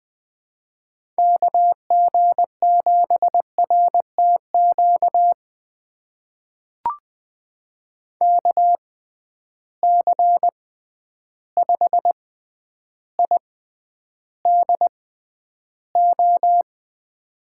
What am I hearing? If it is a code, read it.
Morse code, heard as a keyed tone, about 20 wpm, spaced as for 10 wpm, 700 Hz. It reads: KG7RTQ KC5IDO
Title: USA Call Signs - 6 Characters - 6x Character Spacing - 20wpm